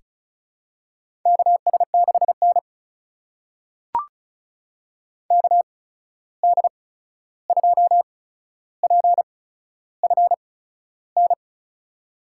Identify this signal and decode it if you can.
Morse code, heard as a keyed tone, about 36 wpm, spaced as for 13 wpm, 700 Hz. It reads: KS6N KD2PFN
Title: USA Call Signs - 8x Character Spacing - 35wpm